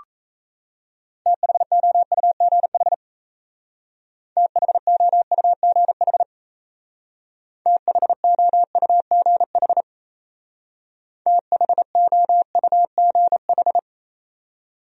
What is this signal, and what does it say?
Morse code, heard as a keyed tone, about 33 wpm, 700 Hz. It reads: THOUGH THOUGH THOUGH THOUGH
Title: Speed-Racing Top 500 Words 28wpm